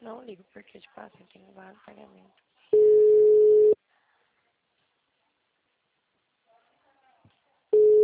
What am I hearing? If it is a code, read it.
Ringback tone (Europe, 425 Hz), with speech.